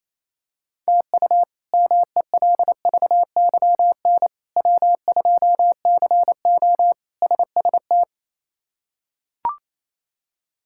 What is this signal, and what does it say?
Morse code, heard as a keyed tone, about 28 wpm, 700 Hz. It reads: TU MELVYN W2CO SST
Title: K1USN SST QSO Practice - 28wpm